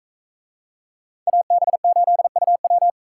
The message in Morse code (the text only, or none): AB8UW